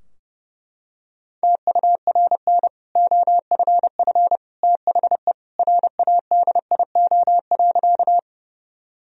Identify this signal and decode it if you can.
Morse code, heard as a keyed tone, about 30 wpm, 700 Hz. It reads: TURN OFF THE RADIO.
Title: Sentences from Top 1000 Words - Review Part 3 - 30wpm